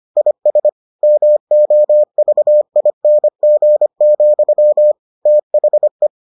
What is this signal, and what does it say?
Morse code, heard as a keyed tone, about 25 wpm, 600 Hz. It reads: IS MOVING, THE